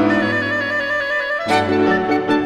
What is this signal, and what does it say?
There is no code.